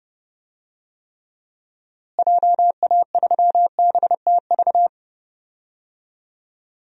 Morse code, heard as a keyed tone, about 30 wpm, 700 Hz. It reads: JA3BTV
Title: DXCC-100 Call Signs - 6 Characters - Repeated 3x - 30wpm